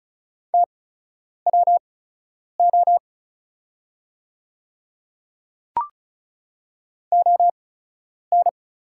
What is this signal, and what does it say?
Morse code, heard as a keyed tone, about 36 wpm, spaced as for 10 wpm, 700 Hz. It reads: TWO ON
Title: 8x Character Spacing - Top 300 Words - 35wpm